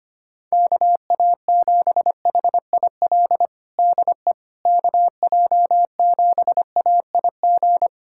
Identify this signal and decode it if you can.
Morse code, heard as a keyed tone, about 25 wpm, 700 Hz. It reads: KA7HIL DE KJ7AIG
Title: USA Callsign DE Callsign - 6 Characters - 25wpm